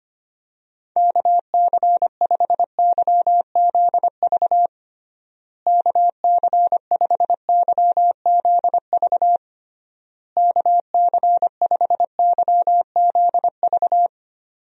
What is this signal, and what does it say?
Morse code, heard as a keyed tone, about 25 wpm, 700 Hz. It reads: KC5YZV KC5YZV KC5YZV
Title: USA Call Signs - 6 Characters - Repeated 3x - 25wpm